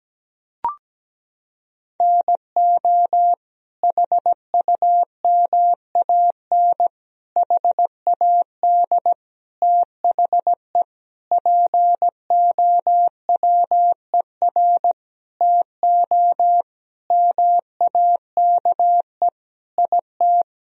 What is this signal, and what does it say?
Morse code, heard as a keyed tone, about 17 wpm, 700 Hz. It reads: NO HUMAN HAD THE POWER TO MAKE IT
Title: Sentences from Top 1000 Words - Review Part 3 - 17wpm